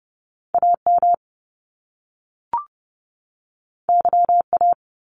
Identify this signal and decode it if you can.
Morse code, heard as a keyed tone, about 30 wpm, 700 Hz. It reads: AM YA
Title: Top Two Letter Combinations in Words 30wpm